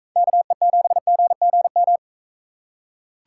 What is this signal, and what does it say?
Morse code, heard as a keyed tone, about 43 wpm, 700 Hz. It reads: KE7GGM